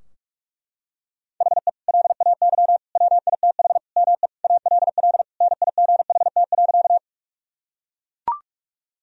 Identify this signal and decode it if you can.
Morse code, heard as a keyed tone, about 41 wpm, 700 Hz. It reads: HE LAY WITH ME ALL NIGHT.
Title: Sentences from Top 1000 Words - Review Part 2 - 45wpm